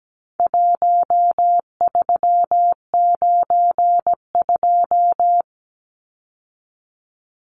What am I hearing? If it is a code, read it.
Morse code, heard as a keyed tone, about 17 wpm, 700 Hz. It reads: 1392
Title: Numbers - Contest Style - 4-Digit 17wpm